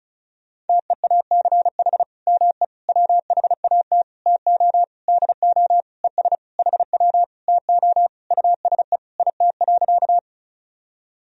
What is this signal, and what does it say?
Morse code, heard as a keyed tone, about 36 wpm, 700 Hz. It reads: TEACH ME WHAT TO DO ES HW TO USE IT.
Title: Sentences from Top 500 Words and Abbreviations 35wpm